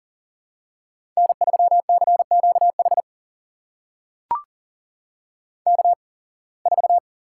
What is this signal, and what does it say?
Morse code, heard as a keyed tone, about 39 wpm, spaced as for 26 wpm, 700 Hz. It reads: N3CQH K4
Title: USA Call Signs - 5 Characters - 8x Character Spacing - 40wpm